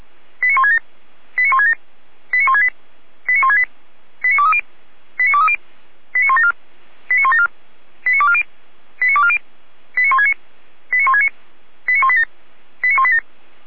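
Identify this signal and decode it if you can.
Five-tone selective call (ZVEI), heard as quick runs of five tones, tones of about 70 ms. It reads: 89167 89167 89168 89168 89230 89230 89174 89174 89240 89240 89169 89169 89187 89187
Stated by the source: ZVEI five-tone call received by SDR.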